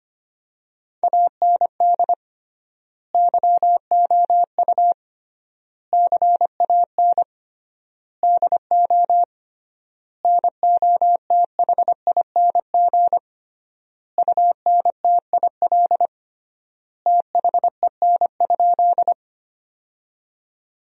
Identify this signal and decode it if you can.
Morse code, heard as a keyed tone, about 25 wpm, 700 Hz. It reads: AND YOU CAN DO NOTHING UNTIL THEN?
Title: 3x Word-Spacing - Sentences from Top 500 Words - 25wpm